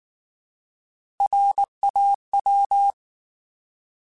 Morse code, heard as a keyed tone, about 19 wpm, 800 Hz. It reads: RAW
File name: Alphabetical / Alphabetical mp3s / Rs @18